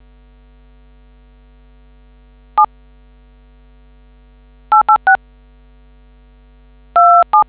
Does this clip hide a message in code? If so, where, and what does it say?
Yes: DTMF from 2.6 to 7.4 s, heard as telephone touch tones, tones of about 110 ms. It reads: 7 886 27